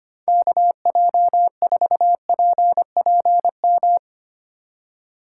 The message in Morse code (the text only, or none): KJ4PPM